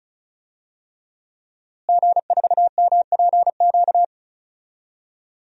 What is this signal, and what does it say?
Morse code, heard as a keyed tone, about 36 wpm, 700 Hz. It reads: G4MPQ
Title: DXCC-100 Call Signs - 5 Characters - Repeated 3x - 35wpm